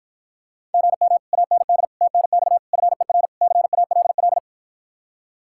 Morse code, heard as a keyed tone, about 52 wpm, 700 Hz. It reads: GM AND TNX FER CALL